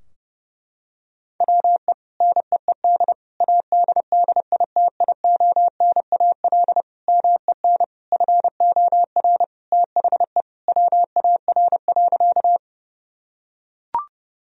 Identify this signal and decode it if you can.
Morse code, heard as a keyed tone, about 32 wpm, 700 Hz. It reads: WE NEED ADDITIONAL MEN FOR THE WAR.
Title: Sentences from Top 1000 Words - Review Part 4 - 30wpm